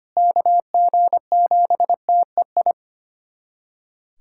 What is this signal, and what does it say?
Morse code, heard as a keyed tone, about 25 wpm, 700 Hz. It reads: KG7TEI
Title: USA Callsign DE Callsign - 6 Characters - 25wpm